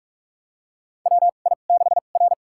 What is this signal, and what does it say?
Morse code, heard as a keyed tone, about 46 wpm, spaced as for 32 wpm, 700 Hz. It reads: WI6R
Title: USA Call Signs - 4 Characters - 2x Character Spacing - 45wpm